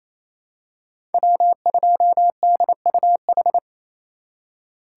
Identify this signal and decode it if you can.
Morse code, heard as a keyed tone, about 28 wpm, 700 Hz. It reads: W2DUH
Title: USA Call Signs - 5 Characters - 4x Character Spacing - 28wpm